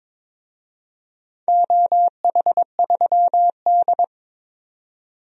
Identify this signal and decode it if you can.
Morse code, heard as a keyed tone, about 22 wpm, 700 Hz. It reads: OH3D
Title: DXCC-100 Call Signs - 4 Characters - 22wpm